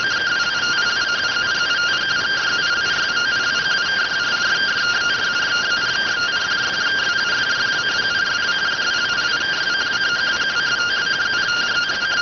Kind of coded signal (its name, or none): two-tone teleprinter signal